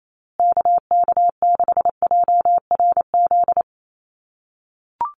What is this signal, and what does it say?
Morse code, heard as a keyed tone, about 28 wpm, 700 Hz. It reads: KK6JRZ
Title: USA Call Signs Repeated 2x 28wpm